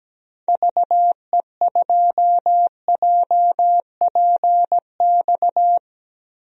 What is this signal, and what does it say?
Morse code, heard as a keyed tone, about 17 wpm, 700 Hz. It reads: VE2JPX
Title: DXCC-100 Call Signs - 6 Characters - Repeated 2x - 17wpm